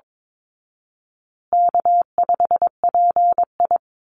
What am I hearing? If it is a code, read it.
Morse code, heard as a keyed tone, about 22 wpm, 700 Hz. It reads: K5PI
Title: Easy Call Signs Repeated 3x - 22wpm